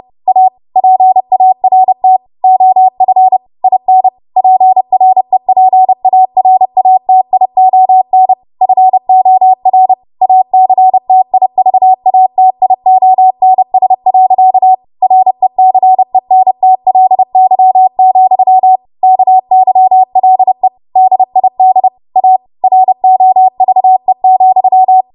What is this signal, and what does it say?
Morse code, heard as a keyed tone, about 30 wpm, 750 Hz. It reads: A PART OF IN PREPARATION FOR ACTIVATIONS. RECENTLY, KYLE DID A ROVE,
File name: Morse / 30WPM